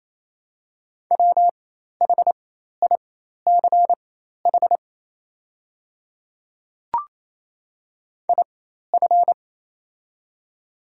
Morse code, heard as a keyed tone, about 28 wpm, spaced as for 13 wpm, 700 Hz. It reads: WHICH IF
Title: 4x Character Spacing - Top 100 Words - 28wpm